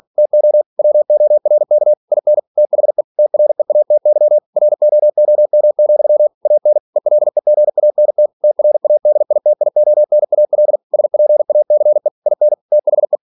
Morse code, heard as a keyed tone, about 46 wpm, 600 Hz. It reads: TO WORK IN THE TREATY ROOM, AN ELEGANT TRADITIONAL SPACE IN THE
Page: Never 47wpm